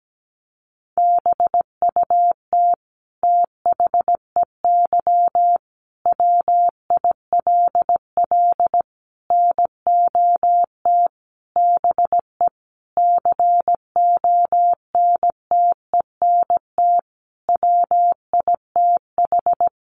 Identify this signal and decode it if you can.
Morse code, heard as a keyed tone, about 17 wpm, 700 Hz. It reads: BUT THEY WILL NOT BE CONTENT WITH
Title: Sentences from Top 900 Words 17wpm